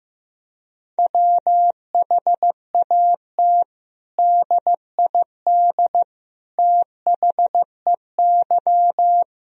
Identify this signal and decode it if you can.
Morse code, heard as a keyed tone, about 15 wpm, 700 Hz. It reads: WHAT DID THEY
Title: Sentences from Top 1000 Words - Review Part 4 - 15wpm